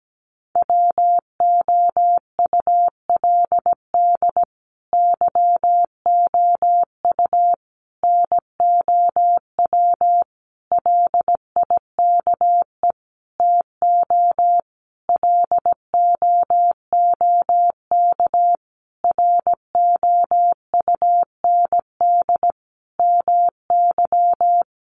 Morse code, heard as a keyed tone, about 17 wpm, 700 Hz. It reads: WOULD YOU NOW LIKE TO LOOK ROUND MY
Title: Sentences from Top 200 Words 17wpm